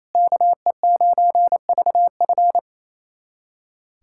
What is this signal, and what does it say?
Morse code, heard as a keyed tone, about 28 wpm, 700 Hz. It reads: KE9VF
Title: USA Callsign DE Callsign - 5 Characters - 28wpm